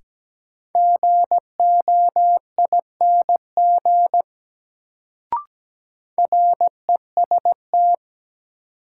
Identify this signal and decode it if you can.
Morse code, heard as a keyed tone, about 17 wpm, 700 Hz. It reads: GOING REST
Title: Sentences from Top 500 Words - New Word Drill - 17wpm